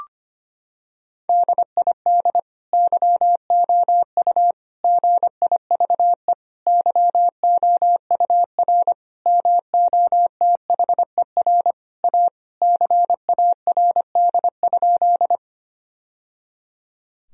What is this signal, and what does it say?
Morse code, heard as a keyed tone, about 25 wpm, 700 Hz. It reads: DID YOU GIVE YOUR MOTHER A CARD?
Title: Sentences from Top 1000 Words - Review Part 4 - 25wpm